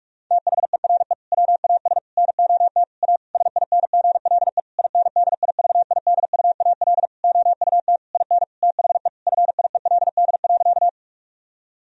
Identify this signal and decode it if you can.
Morse code, heard as a keyed tone, about 46 wpm, 700 Hz. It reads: THERE WAS NOT A SINGLE INDIVIDUAL OUT IN THE FIELD.
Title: Sentences from Top 1000 Words - Review Part 4 - 45wpm